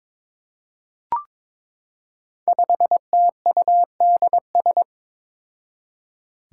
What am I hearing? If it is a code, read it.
Morse code, heard as a keyed tone, about 22 wpm, 700 Hz. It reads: 5TUDS
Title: Five Challenging Letter-Number Combinations 22wpm